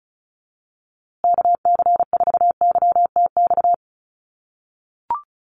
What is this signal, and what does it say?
Morse code, heard as a keyed tone, about 34 wpm, 700 Hz. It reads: KC4YTX
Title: USA Call Signs Repeated 3x 35wpm